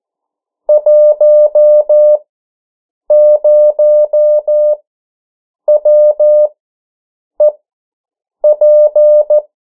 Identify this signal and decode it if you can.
Morse code, heard as a keyed tone, about 14 wpm, spaced as for 7 wpm, 600 Hz. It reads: 10WEP
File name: morse mp3 / 09 urok 8